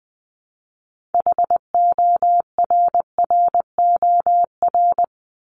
Morse code, heard as a keyed tone, about 20 wpm, 700 Hz. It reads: HORROR
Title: Top 1501-1750 Words 20wpm